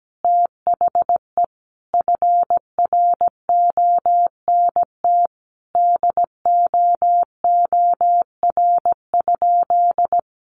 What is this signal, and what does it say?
Morse code, heard as a keyed tone, about 17 wpm, 700 Hz. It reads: THE FRONT DOOR?
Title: Sentences from Top 1300 Words 17wpm